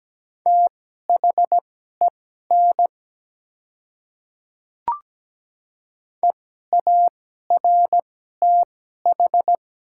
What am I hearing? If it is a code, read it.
Morse code, heard as a keyed tone, about 17 wpm, spaced as for 12 wpm, 700 Hz. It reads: THEN EARTH
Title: 2x Character Spacing - Top 200 Words - 17wpm